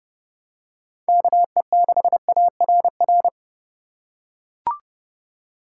Morse code, heard as a keyed tone, about 32 wpm, 700 Hz. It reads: KE6ARR
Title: USA Call Signs - 6 Characters - 2x Character Spacing - 30wpm